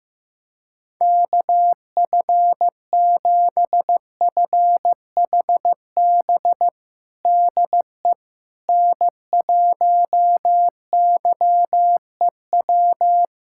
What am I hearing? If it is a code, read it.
Morse code, heard as a keyed tone, about 15 wpm, 700 Hz. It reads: KF7FHB DE N1YEW
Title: USA Callsign DE Callsign 15wpm